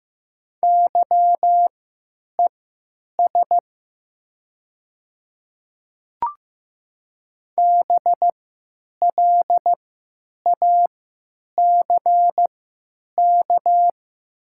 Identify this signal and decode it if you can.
Morse code, heard as a keyed tone, about 15 wpm, spaced as for 9 wpm, 700 Hz. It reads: YES BLACK
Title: 3x Character Spacing - Top 500 Words - 15wpm